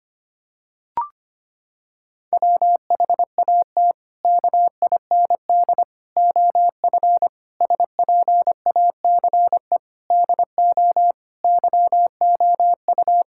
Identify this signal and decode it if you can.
Morse code, heard as a keyed tone, about 25 wpm, 700 Hz. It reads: WHAT KIND OF SPACE DO YOU
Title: Sentences from Top 1000 Words - Review Part 5 - 25wpm